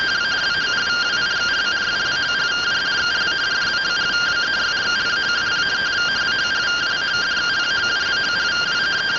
Two-tone teleprinter signal, about 170 Hz shift.